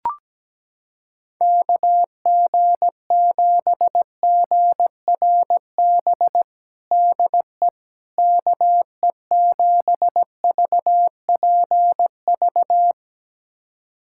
Morse code, heard as a keyed tone, about 17 wpm, 700 Hz. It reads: KG7GRB DE KE7VPV
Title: USA Callsign DE Callsign - 6 Characters - 17wpm